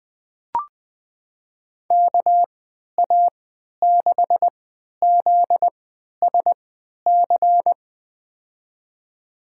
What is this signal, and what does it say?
Morse code, heard as a keyed tone, about 21 wpm, spaced as for 12 wpm, 700 Hz. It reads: KA6ZSC
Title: USA Call Signs - 6 Characters - 3x Character Spacing - 20wpm